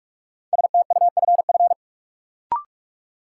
Morse code, heard as a keyed tone, about 44 wpm, 700 Hz. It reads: STUFF
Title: Sentences from Top 1400 Words - New Word Drill - 45wpm